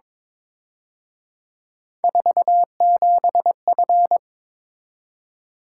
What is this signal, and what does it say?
Morse code, heard as a keyed tone, about 26 wpm, 700 Hz. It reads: 47F
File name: Speed-Racing - Three Letter-Number Combinations - 22wpm